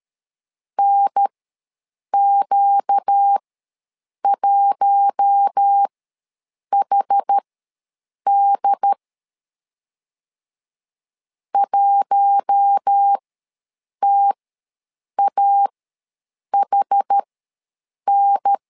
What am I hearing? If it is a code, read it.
Morse code, heard as a keyed tone, about 13 wpm, spaced as for 7 wpm, 800 Hz. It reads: NQ1HD 1TAHN
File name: LEZIONI / 4 Q1 ws 12wpm 2129